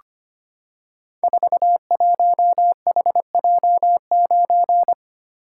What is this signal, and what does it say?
Morse code, heard as a keyed tone, about 25 wpm, 700 Hz. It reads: 41HJ9
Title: Code Groups - 5 Letter:Numbers - 25wpm